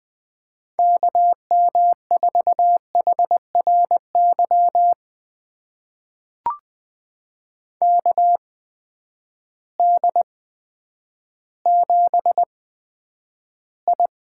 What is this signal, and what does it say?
Morse code, heard as a keyed tone, about 20 wpm, spaced as for 10 wpm, 700 Hz. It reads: KM4HRY KD7I
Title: USA Call Signs - 6 Characters - 8x Character Spacing - 20wpm